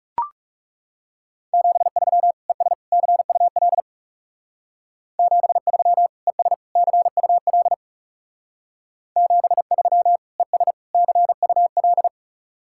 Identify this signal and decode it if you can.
Morse code, heard as a keyed tone, about 35 wpm, 700 Hz. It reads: 73 ES CUL 73 ES CUL 73 ES CUL
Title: Speed-Racing Common QSO Phrases - 30wpm